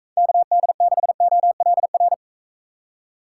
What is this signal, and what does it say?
Morse code, heard as a keyed tone, about 44 wpm, 700 Hz. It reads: KD6OLR